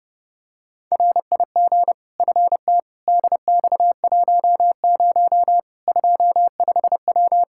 Morse code, heard as a keyed tone, about 32 wpm, 700 Hz. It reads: RIG FT DX10 25W
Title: Practice QSO - Exchange 2 - Info - 30wpm